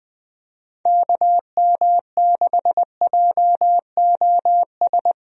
Morse code, heard as a keyed tone, about 20 wpm, 700 Hz. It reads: KM6JOS
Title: USA Call Signs - 6 Characters - Repeated 3x - 20wpm